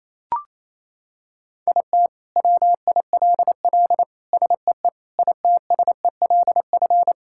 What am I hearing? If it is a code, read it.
Morse code, heard as a keyed tone, about 28 wpm, 700 Hz. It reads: IT WILL SEE ITSELF
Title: Sentences from Top 1000 Words - Review Part 5 - 28wpm